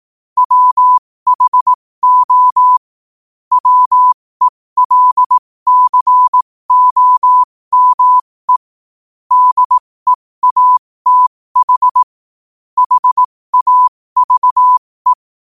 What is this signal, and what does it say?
Morse code, heard as a keyed tone, about 18 wpm, spaced as for 15 wpm, 1000 Hz. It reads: WHO WELCOME DEATH HAVE
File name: qotdfiles / 15wpm